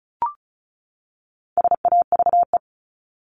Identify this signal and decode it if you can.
Morse code, heard as a keyed tone, about 34 wpm, 700 Hz. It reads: SAVE
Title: Top 1001-1250 Words 35wpm